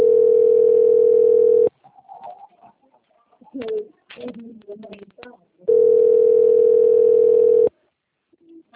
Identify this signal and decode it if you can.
Ringback tone (North America), 440 + 480 Hz, with speech.
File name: out- - - - - .57575